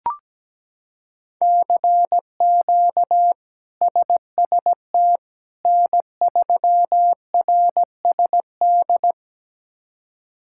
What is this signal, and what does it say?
Morse code, heard as a keyed tone, about 17 wpm, 700 Hz. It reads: CQ SST N3RSD
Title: K1USN SST QSO Practice - 17wpm